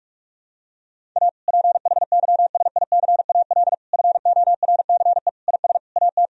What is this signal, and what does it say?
Morse code, heard as a keyed tone, about 41 wpm, 700 Hz. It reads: A PHYSICAL FORCE IS AT
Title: Sentences from Top 1000 Words - Review Part 4 - 45wpm